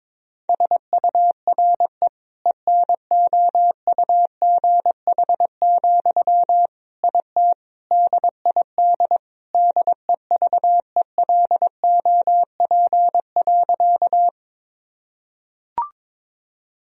Morse code, heard as a keyed tone, about 22 wpm, 700 Hz. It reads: SURE ENOUGH, IT DID DEVELOP.
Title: Sentences from Top 1000 Words - Review Part 2 - 22wpm